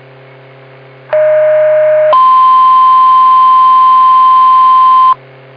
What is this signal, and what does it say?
Two-tone page, heard as a one-second tone, then a three-second tone, about 620 then 1040 Hz. It is heard from 1.1 to 5.1 s.